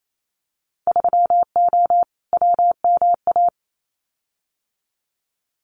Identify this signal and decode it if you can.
Morse code, heard as a keyed tone, about 28 wpm, 700 Hz. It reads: 3O WMA
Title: Winter Field Day Exchanges Encore 28wpm